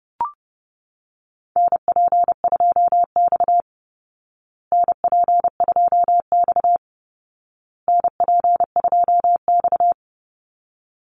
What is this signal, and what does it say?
Morse code, heard as a keyed tone, about 30 wpm, 700 Hz. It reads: NP2X NP2X NP2X
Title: Call Signs Repeated 2x - 30wpm